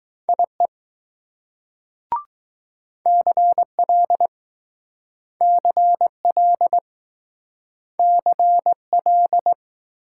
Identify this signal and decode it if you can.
Morse code, heard as a keyed tone, about 20 wpm, 700 Hz. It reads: IE CL CL CL